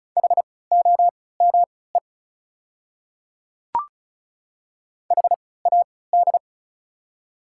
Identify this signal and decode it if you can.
Morse code, heard as a keyed tone, about 34 wpm, spaced as for 20 wpm, 690 Hz. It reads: HOME HAD
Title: 3x Character Spacing - Top 500 Words - 35wpm